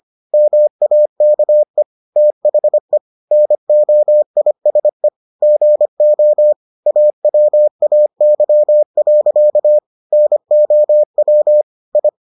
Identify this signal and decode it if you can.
Morse code, heard as a keyed tone, about 25 wpm, 600 Hz. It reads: MAKE THE NOISE GO AWAY. NOW I